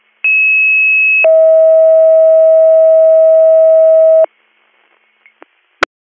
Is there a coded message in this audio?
Yes: two-tone page from 0.2 to 4.3 s, heard as a one-second tone, then a three-second tone, about 2600 then 640 Hz.